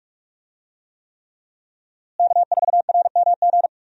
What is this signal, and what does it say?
Morse code, heard as a keyed tone, about 44 wpm, 700 Hz. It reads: K4RMG
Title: USA Call Signs - 5 Characters - Repeated 3x - 45wpm